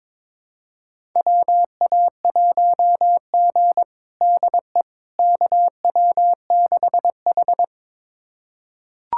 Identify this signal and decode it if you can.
Morse code, heard as a keyed tone, about 22 wpm, 700 Hz. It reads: WA1G DE KW6H
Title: USA Callsign DE Callsign - 4 Characters - 22wpm